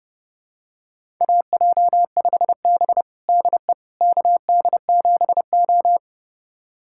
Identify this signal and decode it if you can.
Morse code, heard as a keyed tone, about 30 wpm, 700 Hz. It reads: AJ5B DE KD7O
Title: USA Callsign DE Callsign - 4 Characters - 30wpm